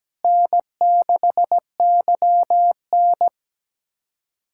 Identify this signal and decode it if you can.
Morse code, heard as a keyed tone, about 17 wpm, 700 Hz. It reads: N6YN
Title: USA Callsign DE Callsign - 4 Characters - 17wpm